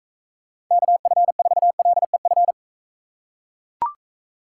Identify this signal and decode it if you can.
Morse code, heard as a keyed tone, about 40 wpm, 700 Hz. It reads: KF4LEF